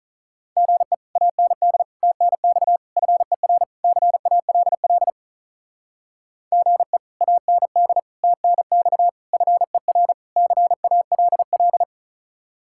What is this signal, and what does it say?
Morse code, heard as a keyed tone, about 45 wpm, 700 Hz. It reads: GE AND TNX FER CALL GE AND TNX FER CALL